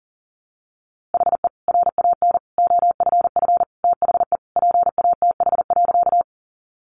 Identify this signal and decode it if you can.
Morse code, heard as a keyed tone, about 40 wpm, 700 Hz. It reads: HE RAN OFF THE PATH.
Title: A Sound Of Thunder 40wpm